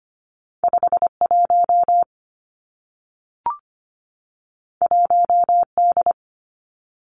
Morse code, heard as a keyed tone, about 25 wpm, 700 Hz. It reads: 51 1D
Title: Two Challenging Letter-Number Combinations 25wpm